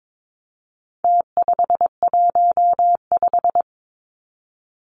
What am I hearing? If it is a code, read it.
Morse code, heard as a keyed tone, about 22 wpm, 700 Hz. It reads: T515
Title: Numbers - Contest Style - 4-Digit 22wpm